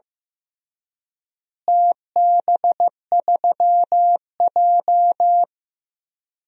Morse code, heard as a keyed tone, about 15 wpm, 700 Hz. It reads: TB3J
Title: DXCC-100 Call Signs - 4 Characters - Repeated 3x - 15wpm